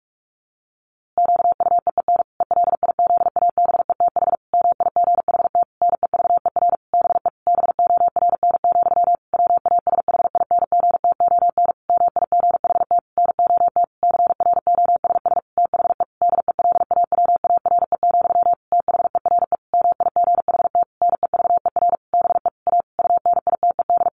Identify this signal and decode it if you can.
Morse code, heard as a keyed tone, about 44 wpm, 700 Hz. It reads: QUEEN ELIZABETH MIGHT NEVER BE BORN, WASHINGTON MIGHT NOT CROSS THE DELAWARE, THERE MIGHT NEVER BE A UNITED
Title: A Sound Of Thunder 45wpm